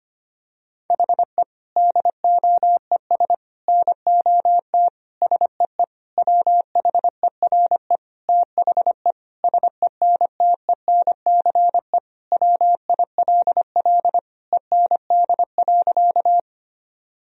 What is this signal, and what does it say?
Morse code, heard as a keyed tone, about 25 wpm, 700 Hz. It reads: HE DOES NOT SEE WHERE THE SENTENCE WILL END.
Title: Sentences from Top 1000 Words - Review Part 1 - 25wpm